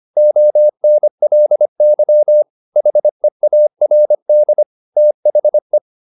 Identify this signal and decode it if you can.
Morse code, heard as a keyed tone, about 25 wpm, 600 Hz. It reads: ONLY HEARD THE